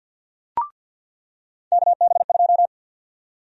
Morse code, heard as a keyed tone, about 53 wpm, 700 Hz. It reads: KB2
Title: USA Call Sign Prefixes - 50wpm